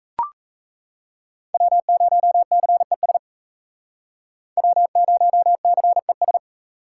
Morse code, heard as a keyed tone, about 35 wpm, 700 Hz. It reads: W0CES W0CES
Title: Speed Racing - USA 1x3 Call Signs 28wpm